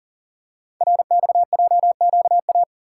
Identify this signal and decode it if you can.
Morse code, heard as a keyed tone, about 39 wpm, 700 Hz. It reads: RXJQA
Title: Code Groups - 5 Letters - 40wpm